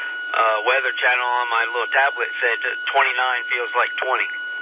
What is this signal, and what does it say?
Emergency Alert System (SAME) tones, heard as harsh buzzing bursts, with speech. It is heard from 0.8 to 3.7 s.